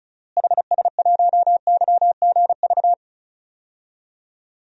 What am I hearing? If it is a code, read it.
Morse code, heard as a keyed tone, about 34 wpm, 700 Hz. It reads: HS1YGV
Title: DXCC-100 Call Signs - 6 Characters - Repeated 2x - 35wpm